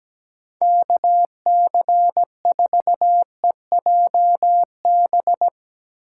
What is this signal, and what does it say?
Morse code, heard as a keyed tone, about 17 wpm, 700 Hz. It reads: KC4EJB
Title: USA Call Signs - 6 Characters - 8x Character Spacing - 17wpm